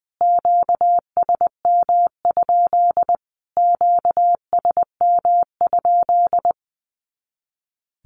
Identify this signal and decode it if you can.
Morse code, heard as a keyed tone, about 20 wpm, 700 Hz. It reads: QSM? QSM?
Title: Q-Codes Repeated 3x 20wpm